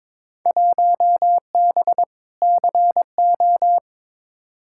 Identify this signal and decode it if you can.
Morse code, heard as a keyed tone, about 22 wpm, 700 Hz. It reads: 1B CO
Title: Field Day Exchanges - 22wpm